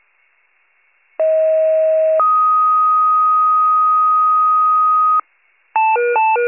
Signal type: two-tone page

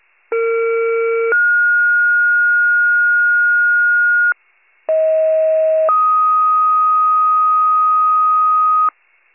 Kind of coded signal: two-tone page

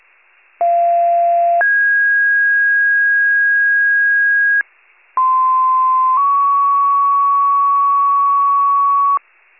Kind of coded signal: two-tone page

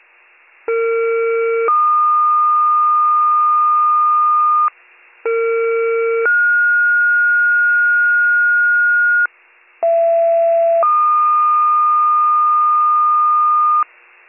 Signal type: two-tone page

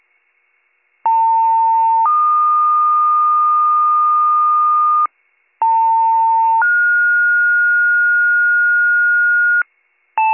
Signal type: two-tone page

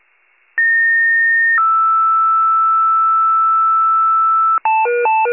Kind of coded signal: two-tone page

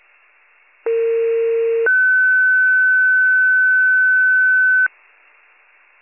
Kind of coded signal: two-tone page

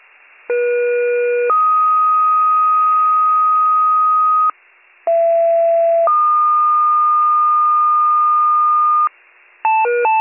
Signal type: two-tone page

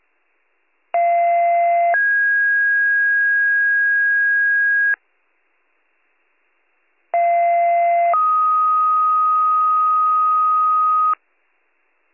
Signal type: two-tone page

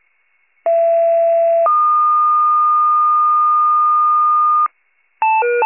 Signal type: two-tone page

